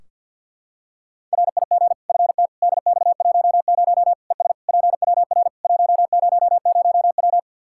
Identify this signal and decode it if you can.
Morse code, heard as a keyed tone, about 46 wpm, 700 Hz. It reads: RIG FT DX10 ES PWR 100W
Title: K1USN SST QSO Practice 50wpm